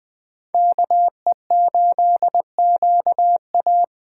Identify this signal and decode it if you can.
Morse code, heard as a keyed tone, about 20 wpm, 700 Hz. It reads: KE8QA
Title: Speed Racing - USA Call Signs 17wpm